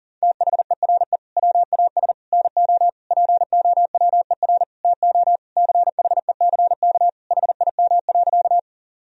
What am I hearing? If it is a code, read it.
Morse code, heard as a keyed tone, about 37 wpm, 700 Hz. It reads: THERE WAS NO POWER TO CHECK HIM.
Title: Sentences from Top 1000 Words - Review Part 3 - 40wpm